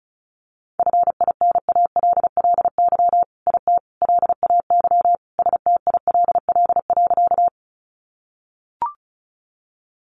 Morse code, heard as a keyed tone, about 34 wpm, 700 Hz. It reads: FINALLY IT LAY STILL.
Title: Sentences from Top 1000 Words 35wpm